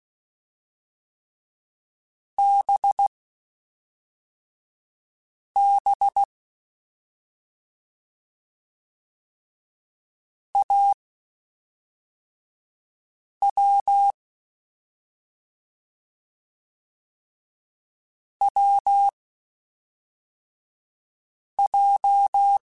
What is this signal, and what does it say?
Morse code, heard as a keyed tone, about 16 wpm, spaced as for 3 wpm, 800 Hz. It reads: BB AW WJ
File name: Short Runs / Short Runs-mp3s / Unballanced @5